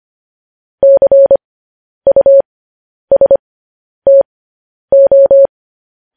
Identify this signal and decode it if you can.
Morse code, heard as a keyed tone, about 25 wpm, spaced as for 10 wpm, 550 Hz. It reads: CUSTO
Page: news-headlines-in-morse-code-at-10-wpm